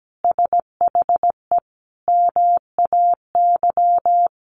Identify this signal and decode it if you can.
Morse code, heard as a keyed tone, about 17 wpm, 700 Hz. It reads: SHE MAY
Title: Sentences from Top 1400 Words 17wpm